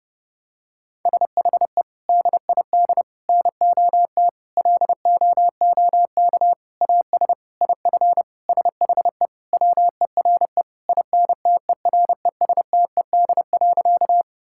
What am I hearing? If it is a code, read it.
Morse code, heard as a keyed tone, about 31 wpm, 700 Hz. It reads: SHE DID NOT LOOK AS IF SHE WERE INTERESTED.
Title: Sentences from Top 1000 Words - Review Part 6 - 30wpm